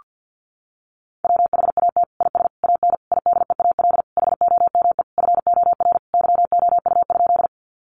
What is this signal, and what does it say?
Morse code, heard as a keyed tone, about 50 wpm, 700 Hz. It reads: WHAT IS AN IDEAL HOME FOR YOU?
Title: Sentences from Top 1400 Words 50wpm